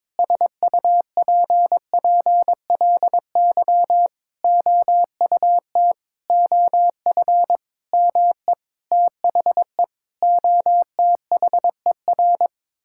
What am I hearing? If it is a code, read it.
Morse code, heard as a keyed tone, about 22 wpm, 700 Hz. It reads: SUPPLY OUT OF ME THE OTHER
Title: Sentences from Top 1000 Words - Review Part 5 - 22wpm